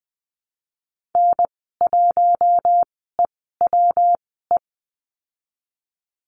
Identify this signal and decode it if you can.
Morse code, heard as a keyed tone, about 20 wpm, spaced as for 15 wpm, 700 Hz. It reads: N1EWE
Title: 154 Lesson 25g - Letter p - Call Signs 20wpm